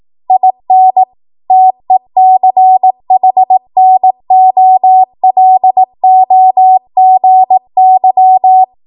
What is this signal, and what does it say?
Morse code, heard as a keyed tone, about 18 wpm, 750 Hz. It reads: IN TECHNOLOGY
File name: Morse / 18WPM